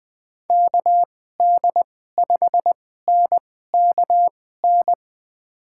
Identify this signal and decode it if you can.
Morse code, heard as a keyed tone, about 20 wpm, spaced as for 15 wpm, 700 Hz. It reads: KD5NKN
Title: USA Call Signs - 6 Characters - 2x Character Spacing - 20wpm